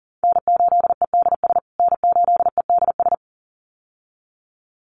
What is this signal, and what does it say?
Morse code, heard as a keyed tone, about 40 wpm, 700 Hz. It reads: N8EDS N8EDS
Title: USA Call Signs Repeated 2x 40wpm